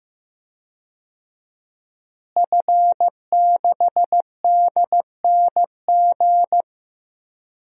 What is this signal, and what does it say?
Morse code, heard as a keyed tone, about 15 wpm, 700 Hz. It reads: F6DNG
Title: DXCC-100 Call Signs - 5 Characters - Repeated 3x - 15wpm